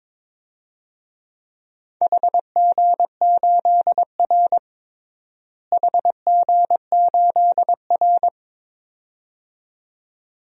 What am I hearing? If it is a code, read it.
Morse code, heard as a keyed tone, about 22 wpm, 700 Hz. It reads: HG8R HG8R
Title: DXCC-100 Call Signs - 4 Characters - Repeated 2x - 22wpm